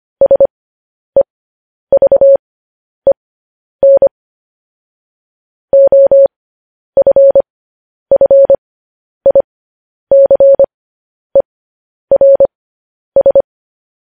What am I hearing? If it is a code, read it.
Morse code, heard as a keyed tone, about 25 wpm, spaced as for 10 wpm, 550 Hz. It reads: SEVEN OFFICERS
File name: podcast / News headlines-1x-25x10